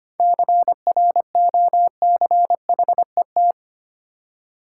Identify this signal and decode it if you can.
Morse code, heard as a keyed tone, about 28 wpm, 700 Hz. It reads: CROCHET
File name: practice / 25wpm / Difficult Words - Speed-Racing with Hints - 25wpm